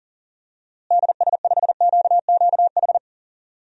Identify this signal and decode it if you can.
Morse code, heard as a keyed tone, about 40 wpm, 700 Hz. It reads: DS5QQH
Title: DXCC-100 Call Signs - 6 Characters - Repeated 3x - 40wpm